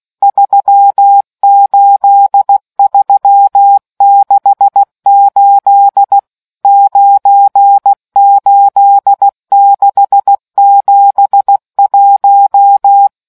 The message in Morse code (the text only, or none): 38368 98671